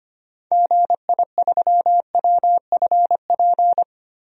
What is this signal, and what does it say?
Morse code, heard as a keyed tone, about 25 wpm, 700 Hz. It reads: GI3WFP
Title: DXCC-100 Call Signs - 6 Characters - 25wpm